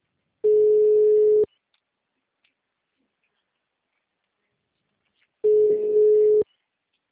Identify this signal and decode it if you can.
Ringback tone (Europe, 425 Hz).